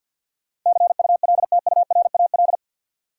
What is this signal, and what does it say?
Morse code, heard as a keyed tone, about 53 wpm, 700 Hz. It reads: CULTURAL
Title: Sentences from Top 1500 Words - New Word Drill - 50wpm